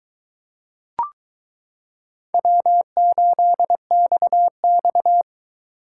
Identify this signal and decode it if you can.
Morse code, heard as a keyed tone, about 22 wpm, 700 Hz. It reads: W8XX